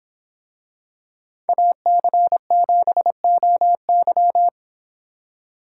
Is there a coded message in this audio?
Yes: Morse code from 1.5 to 4.5 s, heard as a keyed tone, about 24 wpm, 700 Hz. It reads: AC7OY